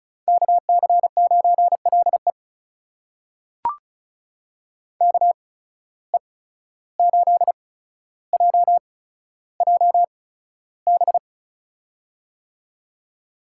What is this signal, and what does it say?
Morse code, heard as a keyed tone, about 36 wpm, spaced as for 14 wpm, 700 Hz. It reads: KC9LE KE8JJB
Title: USA Call Signs - 8x Character Spacing - 35wpm